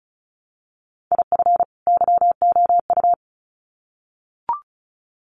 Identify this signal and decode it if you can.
Morse code, heard as a keyed tone, about 34 wpm, 700 Hz. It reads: IF YOU
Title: Sets of 2 Words 35wpm